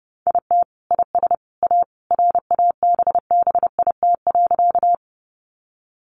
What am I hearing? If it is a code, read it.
Morse code, heard as a keyed tone, about 30 wpm, 700 Hz. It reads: IT IS A RABBIT.
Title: Easy Phrases 30wpm